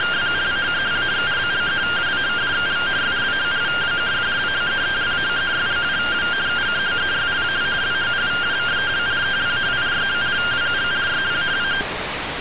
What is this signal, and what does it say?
Two-tone teleprinter signal, about 170 Hz shift.